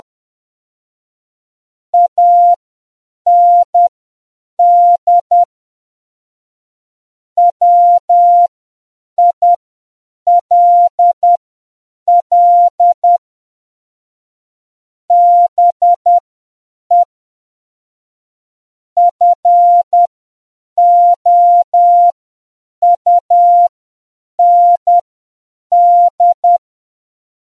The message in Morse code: AND WILL BE FOUND